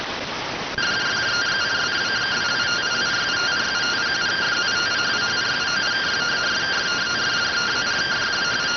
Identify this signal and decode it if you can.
Two-tone teleprinter signal, about 170 Hz shift.